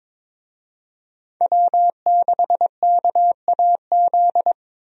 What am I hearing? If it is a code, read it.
Morse code, heard as a keyed tone, about 22 wpm, 700 Hz. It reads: W6KAZ
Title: USA Call Signs - 5 Characters - Repeated 3x - 22wpm